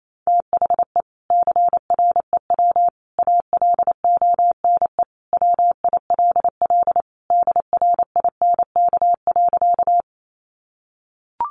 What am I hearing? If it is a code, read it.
Morse code, heard as a keyed tone, about 28 wpm, 700 Hz. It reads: THE CREW ALONE WILL DRINK.
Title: Sentences from Top 1500 Words 28wpm